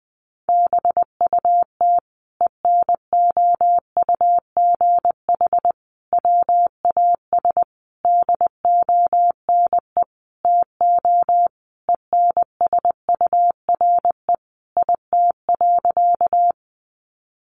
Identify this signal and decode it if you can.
Morse code, heard as a keyed tone, about 20 wpm, 700 Hz. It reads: BUT ENOUGH WAS DONE TO ENSURE IT.
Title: Sentences from Top 1100 Words 20wpm